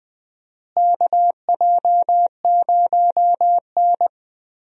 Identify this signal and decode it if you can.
Morse code, heard as a keyed tone, about 20 wpm, 700 Hz. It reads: KJ0N
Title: USA Call Signs - 4 Characters - Repeated 3x - 20wpm